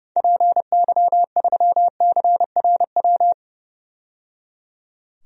Morse code, heard as a keyed tone, about 30 wpm, 700 Hz. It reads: PY3CRW
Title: DXCC-100 Call Signs - 6 Characters - 30wpm